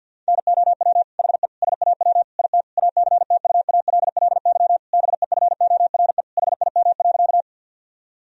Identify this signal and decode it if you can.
Morse code, heard as a keyed tone, about 51 wpm, 700 Hz. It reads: NOW HE SAW IT ACTUALLY BEFORE HIM.
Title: Sentences from Top 1000 Words - Review Part 4 - 50wpm